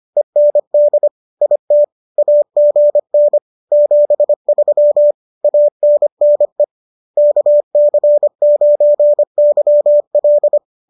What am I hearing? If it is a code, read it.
Morse code, heard as a keyed tone, about 25 wpm, 590 Hz. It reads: END IT AGN 73 ANNE KC9YL